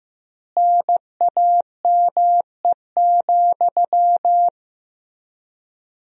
Morse code, heard as a keyed tone, about 15 wpm, 700 Hz. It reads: NAME,
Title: 4x Word-Spacing - Sentences from Top 500 Words - 15wpm